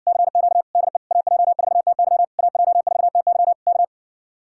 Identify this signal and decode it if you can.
Morse code, heard as a keyed tone, about 60 wpm, 700 Hz. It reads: CQ DE AJ4TY AJ4TY K